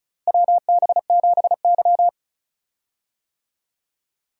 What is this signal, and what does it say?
Morse code, heard as a keyed tone, about 36 wpm, 700 Hz. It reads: WB7Y
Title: USA Call Signs - 4 Characters - Repeated 2x - 35wpm